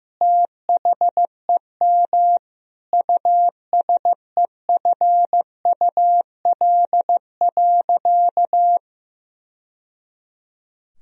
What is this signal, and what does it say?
Morse code, heard as a keyed tone, about 15 wpm, 700 Hz. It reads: THEM USEFUL.
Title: Sentences from Top 1000 Words - Review Part 4 - 15wpm